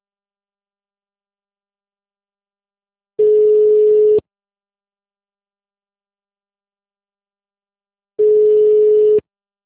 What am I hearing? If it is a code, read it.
Ringback tone (Europe, 425 Hz).